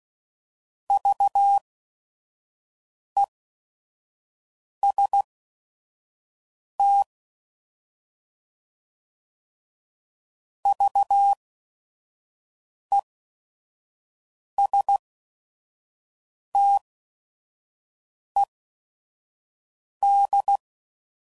Morse code, heard as a keyed tone, about 16 wpm, spaced as for 5 wpm, 800 Hz. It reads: VEST VESTED
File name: Prefixes / By First Letter / Vs